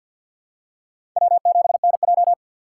Morse code, heard as a keyed tone, about 53 wpm, 700 Hz. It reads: W7NJ
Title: USA Call Signs - 3x Character Spacing - 50wpm